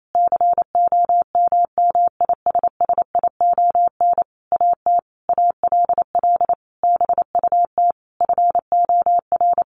Morse code, heard as a keyed tone, about 28 wpm, 700 Hz. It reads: COMMISSION AT ALL BUT FOR
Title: Sentences from Top 1400 Words 28wpm